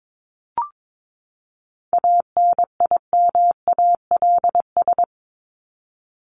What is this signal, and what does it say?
Morse code, heard as a keyed tone, about 22 wpm, 700 Hz. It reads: ANIMALS
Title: ES:S Suffix - Top 100 Words with Base Words 22wpm